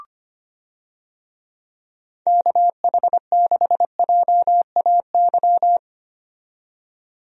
Morse code, heard as a keyed tone, about 25 wpm, 700 Hz. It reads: KH6JAY
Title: DXCC-100 Call Signs - 6 Characters - 25wpm